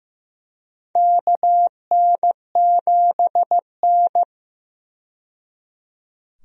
Morse code, heard as a keyed tone, about 15 wpm, 700 Hz. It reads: KN7N
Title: USA Call Signs - 4 Characters - 15wpm